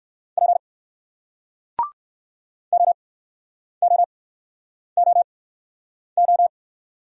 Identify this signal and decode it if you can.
Morse code, heard as a keyed tone, about 58 wpm, spaced as for 9 wpm, 700 Hz. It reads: P OOOO